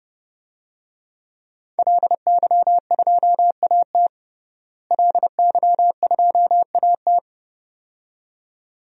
Morse code, heard as a keyed tone, about 30 wpm, 700 Hz. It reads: LY2AT LY2AT
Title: DXCC-100 Call Signs - 5 Characters - Repeated 2x - 30wpm